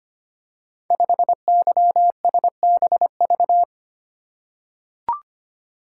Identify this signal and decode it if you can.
Morse code, heard as a keyed tone, about 25 wpm, 700 Hz. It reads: 5YSBV
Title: Code Groups - 5 Letter:Numbers - 25wpm